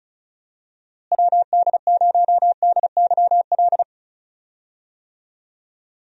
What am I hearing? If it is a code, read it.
Morse code, heard as a keyed tone, about 34 wpm, 700 Hz. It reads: WD0DYL
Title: USA Call Signs - 6 Characters - Repeated 2x - 35wpm